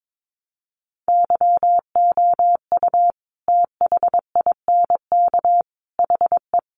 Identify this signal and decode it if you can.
Morse code, heard as a keyed tone, about 22 wpm, 700 Hz. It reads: YOU THINK HE
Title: Sentences from Top 800 Words 22wpm